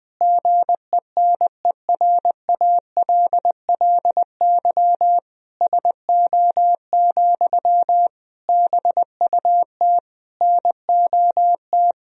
Morse code, heard as a keyed tone, about 20 wpm, 700 Hz. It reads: GENERALLY SO, BUT NOT
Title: Sentences from Top 1000 Words - Review Part 4 - 20wpm